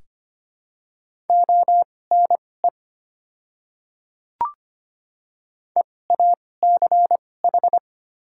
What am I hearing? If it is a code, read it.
Morse code, heard as a keyed tone, about 25 wpm, spaced as for 18 wpm, 700 Hz. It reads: ONE EACH
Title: 2x Character Spacing - Top 100 Words - 25wpm